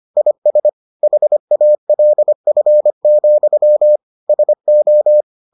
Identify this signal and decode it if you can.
Morse code, heard as a keyed tone, about 25 wpm, 600 Hz. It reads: IS HALF, SO